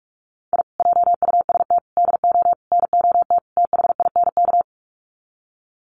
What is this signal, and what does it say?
Morse code, heard as a keyed tone, about 44 wpm, 700 Hz. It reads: I JUST DO NOT THINK
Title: Sets of 5 Words Encore 45wpm